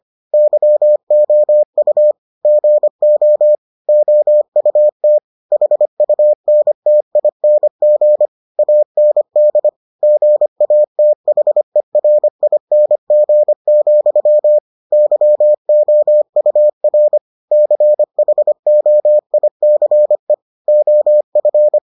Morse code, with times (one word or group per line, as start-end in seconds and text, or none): 0.3-2.1 YOU
2.5-3.5 GO
3.9-5.2 OUT
5.5-8.3 HUNTING
8.6-9.7 AND
10.0-14.6 GATHERING,
14.9-17.2 YOUR
17.5-20.3 CHOICE
20.7-21.8 OF